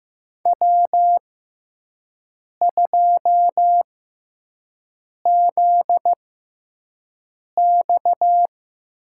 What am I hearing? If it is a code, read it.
Morse code, heard as a keyed tone, about 15 wpm, 700 Hz. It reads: W 2 Z X
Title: USA Call Signs - 4 Characters - 6x Character Spacing - 15wpm